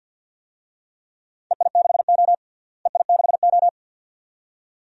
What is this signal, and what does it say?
Morse code, heard as a keyed tone, about 46 wpm, 700 Hz. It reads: EI6O EI6O
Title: DXCC-100 Call Signs - 4 Characters - Repeated 2x - 50wpm